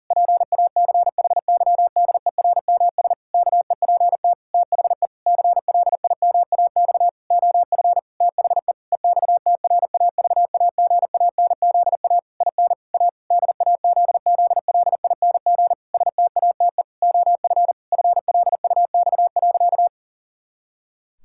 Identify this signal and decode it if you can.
Morse code, heard as a keyed tone, about 40 wpm, 700 Hz. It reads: PACHYDERMS KEPT THE CLIMAX OF THE EXTRAVAGANZA IN A DAZZLING STATE OF FLUX.
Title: Pangram Challenge v2 - 40wpm